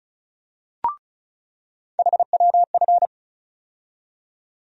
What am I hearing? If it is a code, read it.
Morse code, heard as a keyed tone, about 34 wpm, 700 Hz. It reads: HWF
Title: USA Call Sign Postfixes - 3 Letters 35wpm